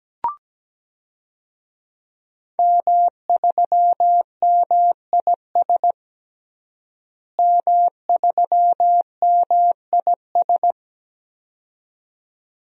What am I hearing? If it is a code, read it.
Morse code, heard as a keyed tone, about 17 wpm, 700 Hz. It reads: M3MIS M3MIS
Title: DXCC-100 Call Signs - 5 Characters - Repeated 2x - 17wpm